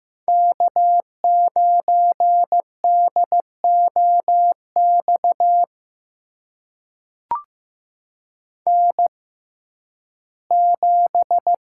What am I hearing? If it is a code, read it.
Morse code, heard as a keyed tone, about 15 wpm, spaced as for 11 wpm, 700 Hz. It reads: K9DOX N7
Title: USA Call Signs - 5 Characters - 6x Character Spacing - 15wpm